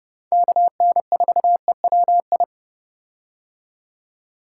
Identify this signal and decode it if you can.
Morse code, heard as a keyed tone, about 30 wpm, 700 Hz. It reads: KN4EWI
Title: K1USN SST QSO Practice - 30wpm